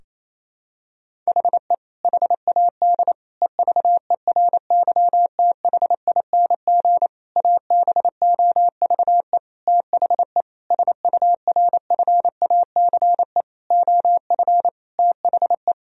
Morse code, heard as a keyed tone, about 27 wpm, 700 Hz. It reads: HE HAD EVERYTHING ABOVE THE SURFACE OF THE
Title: Sentences from Top 1000 Words - Review Part 3 - 28wpm